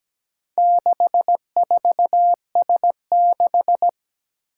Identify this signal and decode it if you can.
Morse code, heard as a keyed tone, about 17 wpm, 700 Hz. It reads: 64S6
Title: Four Challenging Letter-Number Combinations 17wpm